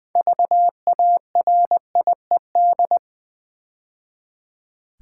Morse code, heard as a keyed tone, about 20 wpm, 700 Hz. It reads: VARIED
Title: Build A Word - 2 Letters At A Time - 6-Letter Words - 20wpm